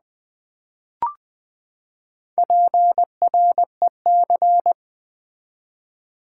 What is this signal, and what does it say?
Morse code, heard as a keyed tone, about 20 wpm, 700 Hz. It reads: PREC
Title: Build A Word - 4 Letters At A Time - 8-Letter Words - 20wpm